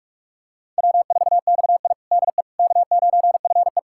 Morse code, heard as a keyed tone, about 41 wpm, 700 Hz. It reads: W4XI DE K9FE
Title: USA Callsign DE Callsign - 4 Characters - 45wpm